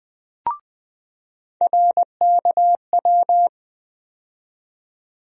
Morse code, heard as a keyed tone, about 20 wpm, 700 Hz. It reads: RKW
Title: USA Call Sign Postfixes - 3 Letters 20wpm